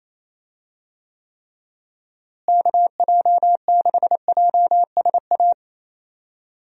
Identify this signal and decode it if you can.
Morse code, heard as a keyed tone, about 27 wpm, 700 Hz. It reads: KJ6JSA
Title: USA Call Signs - 6 Characters - Repeated 3x - 28wpm